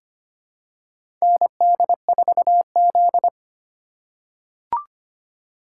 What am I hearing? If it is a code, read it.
Morse code, heard as a keyed tone, about 25 wpm, 700 Hz. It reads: ND4Z
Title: USA Call Signs - 4 Characters - 3x Character Spacing - 25wpm